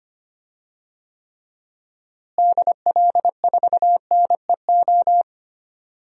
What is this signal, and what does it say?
Morse code, heard as a keyed tone, about 25 wpm, 700 Hz. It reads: DL4NEO
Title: DXCC-100 Call Signs - 6 Characters - Repeated 2x - 25wpm